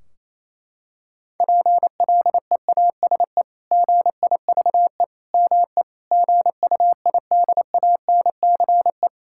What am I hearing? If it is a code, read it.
Morse code, heard as a keyed tone, about 28 wpm, 700 Hz. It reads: PLEASE GIVE ME GUIDANCE
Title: Sentences from Top 1600 Words 28wpm